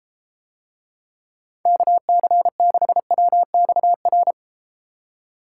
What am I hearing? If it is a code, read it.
Morse code, heard as a keyed tone, about 32 wpm, 700 Hz. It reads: KC6WXR